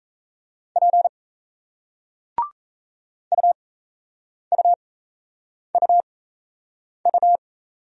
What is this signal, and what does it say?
Morse code, heard as a keyed tone, about 35 wpm, spaced as for 8 wpm, 700 Hz. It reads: P UUUU
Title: Speed-Racing Single Letters 28wpm